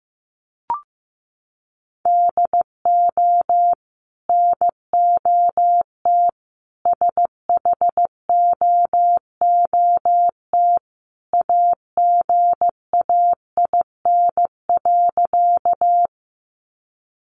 Morse code, heard as a keyed tone, about 15 wpm, 700 Hz. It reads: DO NOT SHOOT AGAIN.
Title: Sentences from Top 1200 Words 15wpm